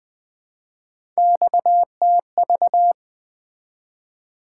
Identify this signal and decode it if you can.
Morse code, heard as a keyed tone, about 21 wpm, 700 Hz. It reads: XTV
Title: Three Letter-Number Combinations - 20wpm